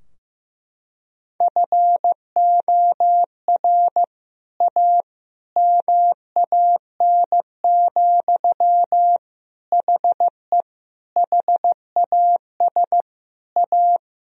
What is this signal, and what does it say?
Morse code, heard as a keyed tone, about 16 wpm, 700 Hz. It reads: FOR A MAN, HE HAS A
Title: Sentences from Top 1000 Words - Review Part 2 - 15wpm